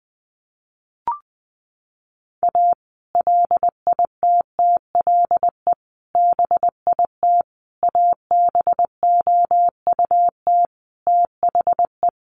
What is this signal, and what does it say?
Morse code, heard as a keyed tone, about 19 wpm, 700 Hz. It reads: A LITTLE BIT ABOUT THE
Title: Sets of 5 Words Encore2 20wpm